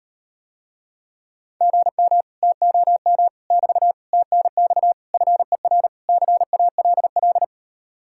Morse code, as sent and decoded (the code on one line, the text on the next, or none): --. -- / - --- -- / -...- / - -. -..- / ..-. .-. / -.-. .- .-.. .-..
GM TOM <BT> TNX FER CALL